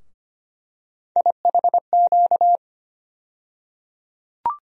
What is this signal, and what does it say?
Morse code, heard as a keyed tone, about 25 wpm, 700 Hz. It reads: IHQ
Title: USA Call Sign Postfixes - 3 Letters 25wpm